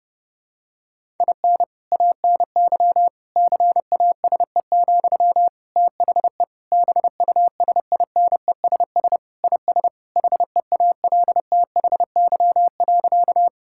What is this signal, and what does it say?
Morse code, heard as a keyed tone, about 32 wpm, 700 Hz. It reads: IN ANY CASE, THE BUSINESS IS HEALTHY.
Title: Sentences from Top 1000 Words - Review Part 5 - 30wpm